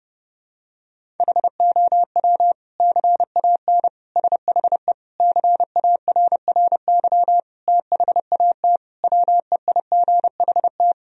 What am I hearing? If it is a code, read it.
Morse code, heard as a keyed tone, about 30 wpm, 700 Hz. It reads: HOW CAN SHE CARRY THAT WEIGHT
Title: Sentences from Top 1000 Words - Review Part 3 - 30wpm